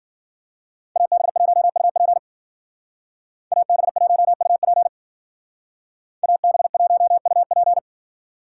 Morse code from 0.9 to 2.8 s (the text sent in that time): AB1UP